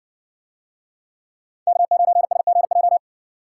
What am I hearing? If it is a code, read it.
Morse code, heard as a keyed tone, about 61 wpm, 700 Hz. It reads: K9SGJ